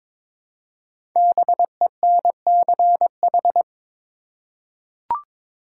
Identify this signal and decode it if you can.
Morse code, heard as a keyed tone, about 22 wpm, 690 Hz. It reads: BENCH
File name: 5 Letter Words - First 3 Letters Given - 22wpm